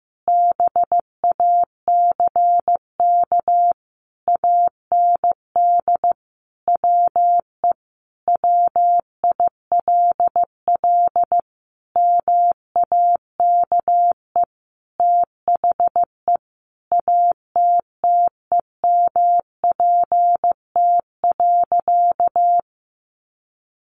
Morse code, heard as a keyed tone, about 15 wpm, 700 Hz. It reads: BACK AND WE WILL MAKE THE ATTEMPT.
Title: Sentences from Top 1000 Words 15wpm